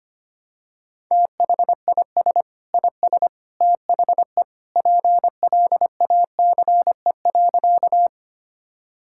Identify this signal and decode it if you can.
Morse code, heard as a keyed tone, about 25 wpm, 700 Hz. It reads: THIS IS THE PLACE.
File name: practice / 25wpm / Sentences with 3-Word Phrases - 25wpm